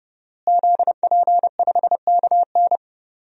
Morse code, heard as a keyed tone, about 30 wpm, 700 Hz. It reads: ZP5KN
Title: DXCC-100 Call Signs - 5 Characters - Repeated 2x - 30wpm